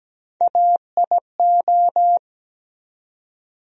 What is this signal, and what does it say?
Morse code, heard as a keyed tone, about 17 wpm, 700 Hz. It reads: AIO
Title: USA Call Sign Postfixes - 3 Letters 17wpm